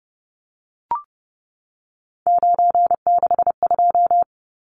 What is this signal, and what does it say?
Morse code, heard as a keyed tone, about 30 wpm, 700 Hz. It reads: 962
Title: Numbers - 3-Digit 30wpm